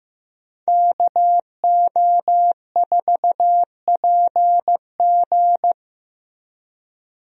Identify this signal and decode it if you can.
Morse code, heard as a keyed tone, about 15 wpm, 700 Hz. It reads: KO4PG
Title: USA Call Signs - 5 Characters - Repeated 3x - 15wpm